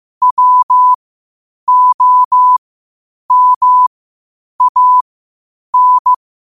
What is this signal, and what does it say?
Morse code, heard as a keyed tone, about 15 wpm, spaced as for 8 wpm, 1000 Hz. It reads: WOMAN
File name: qotdfiles / 8wpm